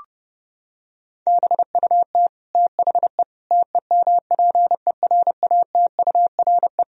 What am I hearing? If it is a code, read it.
Morse code, heard as a keyed tone, about 30 wpm, 700 Hz. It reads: BUT THE TEMPERATURE
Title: Sentences from Top 1000 Words - Review Part 4 - 30wpm